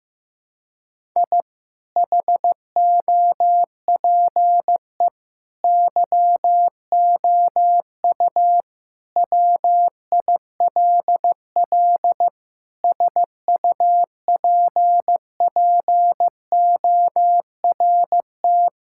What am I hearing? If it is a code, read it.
Morse code, heard as a keyed tone, about 15 wpm, 700 Hz. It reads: I HOPE YOU WILL SUPPORT
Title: Sentences from Top 1000 Words - Review Part 5 - 15wpm